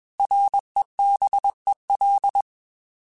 Morse code, heard as a keyed tone, about 21 wpm, 800 Hz. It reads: REBEL
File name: Alphabetical / Alphabetical mp3s / Rs @20